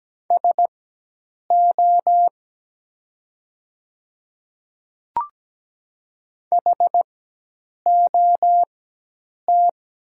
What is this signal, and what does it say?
Morse code, heard as a keyed tone, about 17 wpm, spaced as for 8 wpm, 700 Hz. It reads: SO HOT
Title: 4x Character Spacing - Top 100 Words - 17wpm